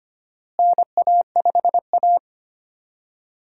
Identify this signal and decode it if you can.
Morse code, heard as a keyed tone, about 25 wpm, 700 Hz. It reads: NA5A
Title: USA Call Signs - 4 Characters - Repeated 2x - 25wpm